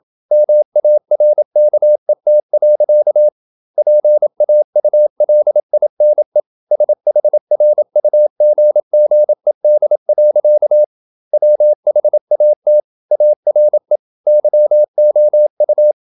Morse code, heard as a keyed tone, about 27 wpm, 600 Hz. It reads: MARKET. PAULINE SHRUGGED. WHAT ARE YOU
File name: Never 0027wpm